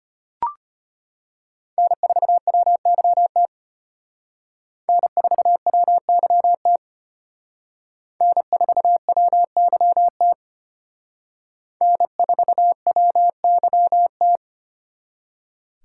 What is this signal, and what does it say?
Morse code, heard as a keyed tone, about 29 wpm, 700 Hz. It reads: N4WYT N4WYT N4WYT N4WYT
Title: Speed Racing - USA 1x3 Call Signs 25wpm